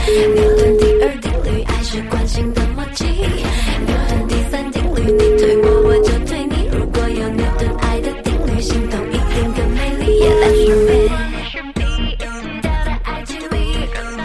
Ringback tone (Europe, 425 Hz), with noise.